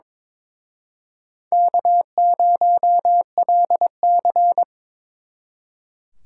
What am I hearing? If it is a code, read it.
Morse code, heard as a keyed tone, about 25 wpm, 700 Hz. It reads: K0LC
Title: Speed Racing - USA 1x2 Call Signs 22wpm